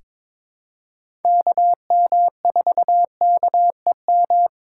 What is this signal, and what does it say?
Morse code, heard as a keyed tone, about 21 wpm, 700 Hz. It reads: KM4KEM
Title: USA Call Signs - 3x Character Spacing - 22wpm